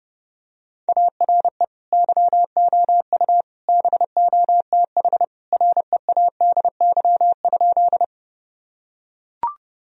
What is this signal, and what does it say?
Morse code, heard as a keyed tone, about 30 wpm, 700 Hz. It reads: ARE YOU BOTH READY?
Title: Sentences from Top 1000 Words - Review Part 1 - 30wpm